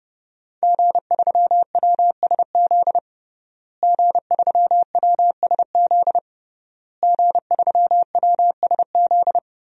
Morse code, heard as a keyed tone, about 30 wpm, 700 Hz. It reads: G3WSZ G3WSZ G3WSZ
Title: DXCC-100 Call Signs - 5 Characters - Repeated 3x - 30wpm